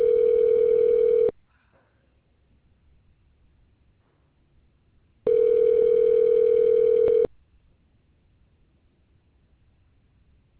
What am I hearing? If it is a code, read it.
Ringback tone (North America), 440 + 480 Hz.